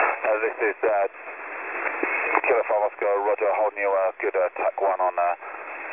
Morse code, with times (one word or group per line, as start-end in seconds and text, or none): none